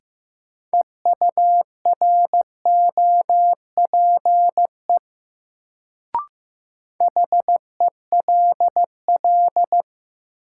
Morse code, heard as a keyed tone, about 15 wpm, 700 Hz. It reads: EUROPE HELL
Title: Sets of 5 Words Encore - New Word Drill - 15wpm